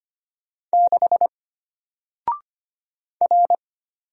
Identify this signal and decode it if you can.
Morse code, heard as a keyed tone, about 25 wpm, 700 Hz. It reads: 6 R
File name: Single-Letter-Number-25wpm